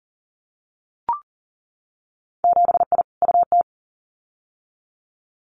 Morse code, heard as a keyed tone, about 42 wpm, 700 Hz. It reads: 7I UT
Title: Winter Field Day Exchanges Encore 40wpm